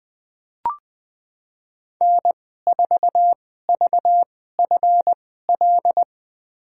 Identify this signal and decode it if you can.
Morse code, heard as a keyed tone, about 20 wpm, spaced as for 15 wpm, 700 Hz. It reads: N4VFL
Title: USA Call Signs - 5 Characters - 2x Character Spacing - 20wpm